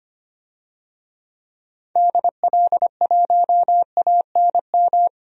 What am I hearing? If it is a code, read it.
Morse code, heard as a keyed tone, about 25 wpm, 700 Hz. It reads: DL1ANM
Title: DXCC-100 Call Signs - 6 Characters - 25wpm